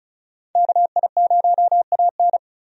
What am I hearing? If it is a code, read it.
Morse code, heard as a keyed tone, about 34 wpm, 700 Hz. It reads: KI0AN
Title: USA Call Signs - 5 Characters - 35wpm